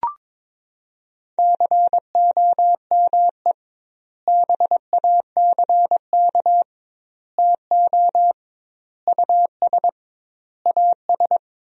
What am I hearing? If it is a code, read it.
Morse code, heard as a keyed tone, about 22 wpm, 700 Hz. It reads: COME BACK TO US AS
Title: 2x Word-Spacing - Sentences from Top 500 Words - 22wpm